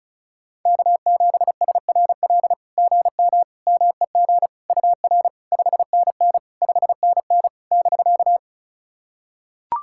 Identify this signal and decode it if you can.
Morse code, heard as a keyed tone, about 36 wpm, 700 Hz. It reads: K7SRL GM MEG UR 5NN 5NN <BK>
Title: SOTA QSO Practice 35wpm